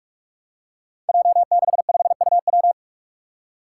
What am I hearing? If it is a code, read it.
Morse code, heard as a keyed tone, about 42 wpm, 700 Hz. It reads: J65UW
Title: Five Challenging Letter-Number Combinations 45wpm